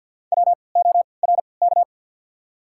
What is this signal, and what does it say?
Morse code, heard as a keyed tone, about 54 wpm, spaced as for 29 wpm, 700 Hz. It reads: WORK
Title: 3x Character Spacing - Top 300 Words - 50wpm